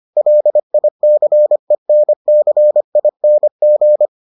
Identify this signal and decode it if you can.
Morse code, heard as a keyed tone, about 25 wpm, 600 Hz. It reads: LICENCING